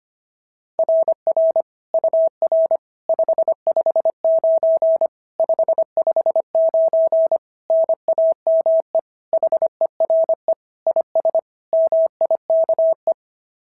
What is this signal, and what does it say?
Morse code, heard as a keyed tone, about 25 wpm, 690 Hz. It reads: RR UR 559 559 NAME HERE IS MIKE
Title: SOTA QSO Practice 25wpm